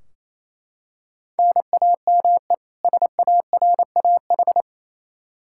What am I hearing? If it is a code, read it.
Morse code, heard as a keyed tone, about 27 wpm, 700 Hz. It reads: NAME SARAH
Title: K1USN SST QSO Practice 28wpm